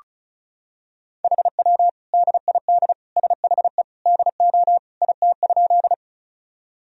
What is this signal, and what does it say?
Morse code, heard as a keyed tone, about 34 wpm, 700 Hz. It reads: HW DID SHE DO IT?
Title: Sentences from Top 100 Words and Abbreviations 35wpm